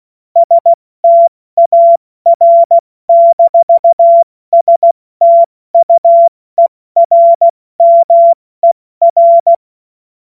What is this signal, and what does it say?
Morse code, heard as a keyed tone, about 16 wpm, 680 Hz. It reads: STAR-STUERMER
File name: Morsekurs / Witze / Witz 27. April 2021, 70 F80